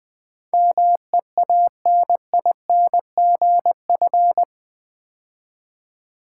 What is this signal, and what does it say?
Morse code, heard as a keyed tone, about 20 wpm, 700 Hz. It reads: MEANINGF
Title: Build A Word - 4 Letters At A Time - 10-Letter Words - 20wpm